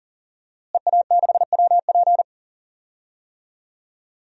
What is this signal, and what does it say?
Morse code, heard as a keyed tone, about 40 wpm, 700 Hz. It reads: EA6WP
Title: DXCC-100 Call Signs - 5 Characters - Repeated 3x - 40wpm